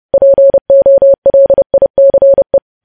Morse code, heard as a keyed tone, about 30 wpm, 550 Hz. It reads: POLICE
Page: news headlines in morse code at 30 wpm